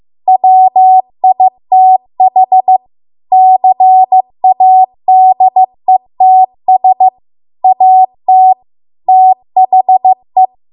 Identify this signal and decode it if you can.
Morse code, heard as a keyed tone, about 15 wpm, 750 Hz. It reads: WITH CADETS AT THE